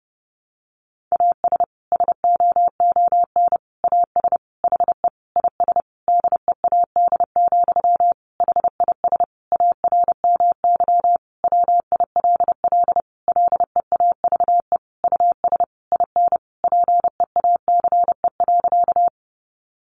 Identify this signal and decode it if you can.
Morse code, heard as a keyed tone, about 30 wpm, 700 Hz. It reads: AS SOON AS HE IS DEAD, HIS ARMY WILL LEAVE US IN PEACE.
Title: Sentences from Top 1500 Words 30wpm